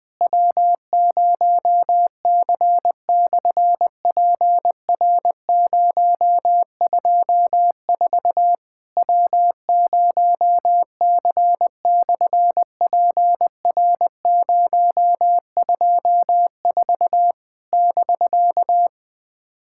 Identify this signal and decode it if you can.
Morse code, heard as a keyed tone, about 20 wpm, 700 Hz. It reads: W0C/PR024 W0C/PR024 <BK>
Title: SOTA QSO Practice 20wpm